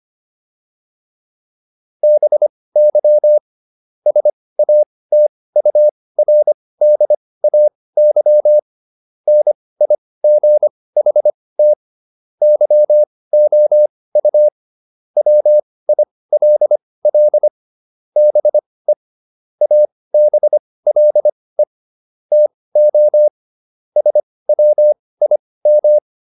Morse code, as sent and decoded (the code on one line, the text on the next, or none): -... -.-- / ... .- - ..- .-. -.. .- -.-- / -. .. --. .... - / -.-- --- ..- / .-- .. .-.. .-.. / -... / .- -... .-.. / - --- / ... .-- .. --
BY SATURDAY NIGHT YOU WILL BE ABLE TO SWIM